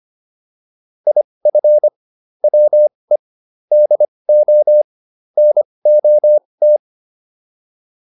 Morse code, heard as a keyed tone, about 25 wpm, spaced as for 20 wpm, 600 Hz. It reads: IF WE DO NOT